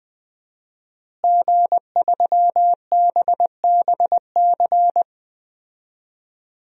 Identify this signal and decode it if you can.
Morse code, heard as a keyed tone, about 21 wpm, 700 Hz. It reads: G3BBC
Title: DXCC-100 Call Signs - 5 Characters - Repeated 3x - 20wpm